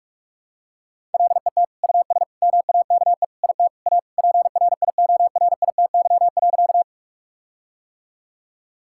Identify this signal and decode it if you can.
Morse code, heard as a keyed tone, about 46 wpm, 700 Hz. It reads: LET US MAKE IT A PRIORITY.
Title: Sentences from Top 1600 Words 45wpm